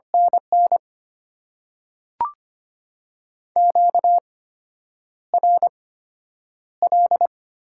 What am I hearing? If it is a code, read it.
Morse code, heard as a keyed tone, about 25 wpm, spaced as for 9 wpm, 690 Hz. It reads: NN QRL
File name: Top 90 QSO Elements - 8x Character Spacing - 25wpm